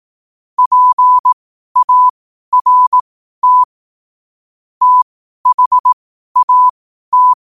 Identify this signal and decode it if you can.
Morse code, heard as a keyed tone, about 18 wpm, spaced as for 12 wpm, 1000 Hz. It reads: PART THAT
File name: qotdfiles / 12wpm